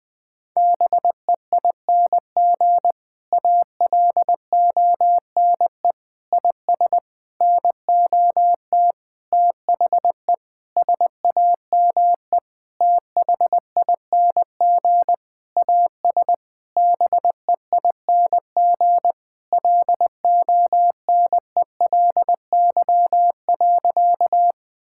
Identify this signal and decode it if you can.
Morse code, heard as a keyed tone, about 20 wpm, 700 Hz. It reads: BEING ALONE IS NOT THE SAME THING AS BEING LONELY.
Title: Sentences with 6-Word Phrases - Spoken Beginning - 20wpm